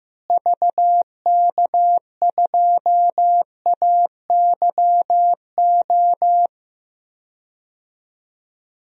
Morse code, heard as a keyed tone, about 16 wpm, 700 Hz. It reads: VK2AYO
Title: DXCC-100 Call Signs - 6 Characters - Repeated 3x - 15wpm